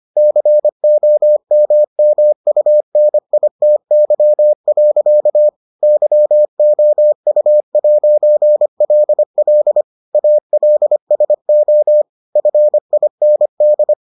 Morse code, heard as a keyed tone, about 25 wpm, 600 Hz. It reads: COMMUNITY. YOU'LL ALSO FIND